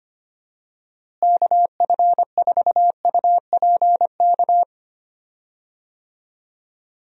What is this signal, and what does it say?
Morse code, heard as a keyed tone, about 25 wpm, 700 Hz. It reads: KF4UPK
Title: USA Call Signs - 6 Characters - Repeated 3x - 25wpm